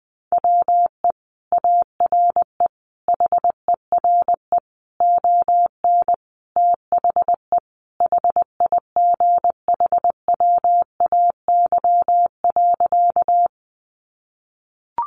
Morse code, heard as a keyed tone, about 20 wpm, 700 Hz. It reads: WE ARE HERE ON THE HIGHWAY.
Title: Sentences from Top 1400 Words 20wpm